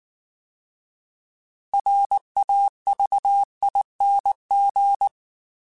Morse code, heard as a keyed tone, about 19 wpm, 800 Hz. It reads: RAVING
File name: Alphabetical / Alphabetical mp3s / Rs @18